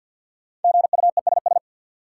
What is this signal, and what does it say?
Morse code, heard as a keyed tone, about 52 wpm, 700 Hz. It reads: GUESS
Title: Sentences from Top 1500 Words - New Word Drill - 50wpm